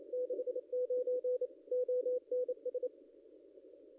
Morse code, heard as a keyed tone, about 28 wpm, 500 Hz. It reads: 69 ONS
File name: contests / ss-cw-k5tr-op-n5rz -0200